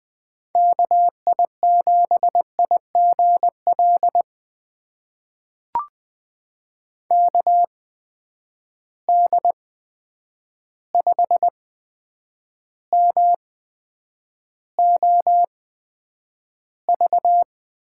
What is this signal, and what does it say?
Morse code, heard as a keyed tone, about 20 wpm, spaced as for 9 wpm, 700 Hz. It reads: KI7IGL KD5MOV
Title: USA Call Signs - 6 Characters - 8x Character Spacing - 20wpm